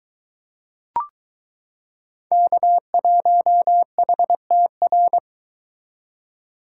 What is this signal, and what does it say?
Morse code, heard as a keyed tone, about 22 wpm, 700 Hz. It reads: K1HTR